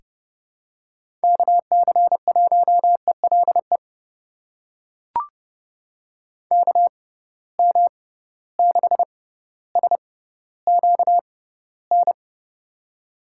Morse code, heard as a keyed tone, about 30 wpm, spaced as for 15 wpm, 700 Hz. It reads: KC1ELE KM6SQN
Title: USA Call Signs - 6 Characters - 6x Character Spacing - 30wpm